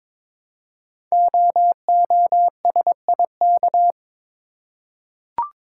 Morse code, heard as a keyed tone, about 22 wpm, 700 Hz. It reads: OOSIK
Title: Code Groups - 5 Letters - 22wpm